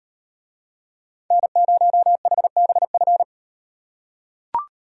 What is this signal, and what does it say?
Morse code, heard as a keyed tone, about 38 wpm, 700 Hz. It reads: N0HBF